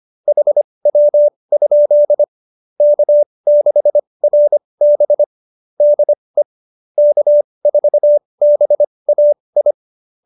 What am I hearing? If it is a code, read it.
Morse code, heard as a keyed tone, about 25 wpm, spaced as for 20 wpm, 590 Hz. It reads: HW? K6RB DE K4BAI
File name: QSO301 20